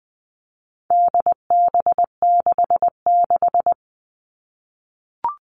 Morse code, heard as a keyed tone, about 20 wpm, 700 Hz. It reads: DB66
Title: All Cut Numbers - 4-Digit 20wpm